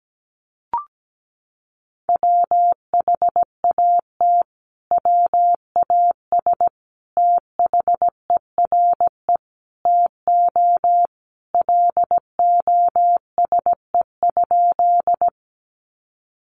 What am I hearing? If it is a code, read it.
Morse code, heard as a keyed tone, about 17 wpm, 700 Hz. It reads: WHAT WAS THERE TO LOSE?
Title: Sentences from Top 1100 Words 17wpm